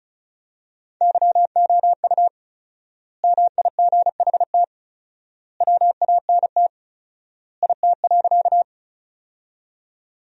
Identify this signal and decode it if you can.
Morse code, heard as a keyed tone, about 36 wpm, 700 Hz. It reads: YOU MIGHT WANT IT.
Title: 4x Word-Spacing - Sentences from Top 300 Words - 35wpm